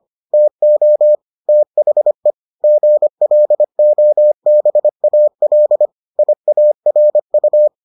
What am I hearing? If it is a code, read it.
Morse code, heard as a keyed tone, about 25 wpm, 600 Hz. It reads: TO THE GLOBAL IARU